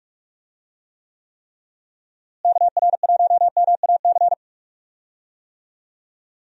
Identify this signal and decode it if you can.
Morse code, heard as a keyed tone, about 46 wpm, 700 Hz. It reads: KR1MAC
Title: USA Call Signs - 6 Characters - Repeated 3x - 45wpm